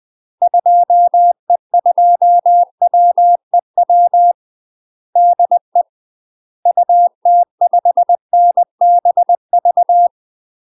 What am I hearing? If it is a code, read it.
Morse code, heard as a keyed tone, about 20 wpm, 700 Hz. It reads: 2E2WEW DE UT5NBV
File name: UK MASTER QSOs TRAINING / 20WPM / UK MASTER QSOs 0052